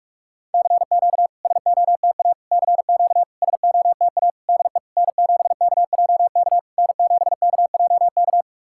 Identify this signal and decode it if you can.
Morse code, heard as a keyed tone, about 48 wpm, 700 Hz. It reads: CQ SOTA CQ SOTA DE N7KJK N7KJK
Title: SOTA QSO Practice 45wpm